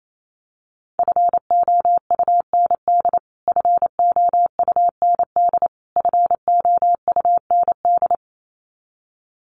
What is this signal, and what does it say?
Morse code, heard as a keyed tone, about 28 wpm, 700 Hz. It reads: FOUND FOUND FOUND
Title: Top 300 Words Repeated 28wpm